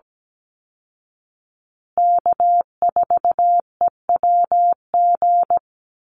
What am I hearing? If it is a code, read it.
Morse code, heard as a keyed tone, about 17 wpm, 700 Hz. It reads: K4EWG
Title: Call Signs Repeated 2x - 17wpm